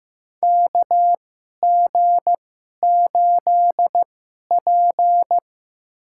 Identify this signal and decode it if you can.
Morse code, heard as a keyed tone, about 15 wpm, spaced as for 11 wpm, 700 Hz. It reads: KG8P
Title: USA Call Signs - 4 Characters - 2x Character Spacing - 15wpm